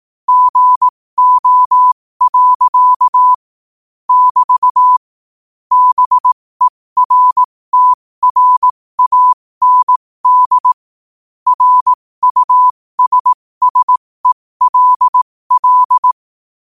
Morse code, heard as a keyed tone, about 18 wpm, spaced as for 15 wpm, 1000 Hz. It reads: GO. <BT> BERTRAND RUSSELL
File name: qotdfiles / 15wpm